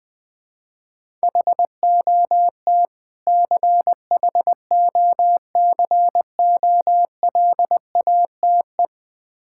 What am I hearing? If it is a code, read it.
Morse code, heard as a keyed tone, about 21 wpm, 700 Hz. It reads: HOT CHOCOLATE
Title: Difficult Words with Hints - 20wpm